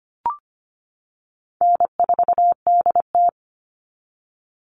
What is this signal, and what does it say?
Morse code, heard as a keyed tone, about 25 wpm, 700 Hz. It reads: N4DT
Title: Easy Call Signs - 25wpm